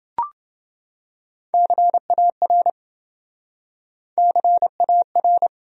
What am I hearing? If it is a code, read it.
Morse code, heard as a keyed tone, about 28 wpm, 700 Hz. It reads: CAR CAR